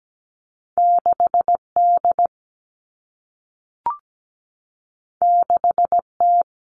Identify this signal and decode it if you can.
Morse code, heard as a keyed tone, about 17 wpm, 700 Hz. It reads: 6D 6T
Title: All Cut Numbers - 2-Digit 17wpm